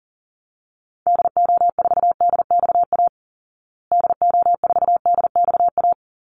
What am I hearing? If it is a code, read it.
Morse code, heard as a keyed tone, about 40 wpm, 700 Hz. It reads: DO4DXA DO4DXA
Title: Call Signs Repeated 3x - 40wpm